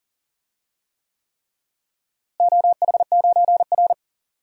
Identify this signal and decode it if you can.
Morse code, heard as a keyed tone, about 40 wpm, 700 Hz. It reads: OH9R
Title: DXCC-100 Call Signs - 4 Characters - Repeated 3x - 40wpm